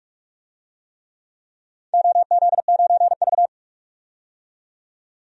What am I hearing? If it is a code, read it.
Morse code, heard as a keyed tone, about 44 wpm, 700 Hz. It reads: OZ9V
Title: DXCC-100 Call Signs - 4 Characters - 45wpm